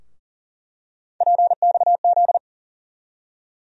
Morse code, heard as a keyed tone, about 40 wpm, 700 Hz. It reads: PXZ
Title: USA Call Sign Postfixes - 3 Letters 40wpm